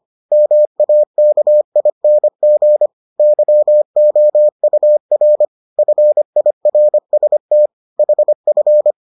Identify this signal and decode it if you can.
Morse code, heard as a keyed tone, about 25 wpm, 600 Hz. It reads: MAKING YOUR FIRST HF